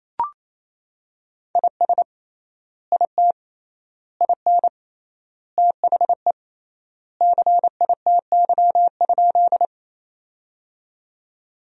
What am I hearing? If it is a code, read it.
Morse code, heard as a keyed tone, about 28 wpm, 700 Hz. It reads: IS IT IN THE CITY?
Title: 3x Word-Spacing - Sentences from Top 300 Words - 28wpm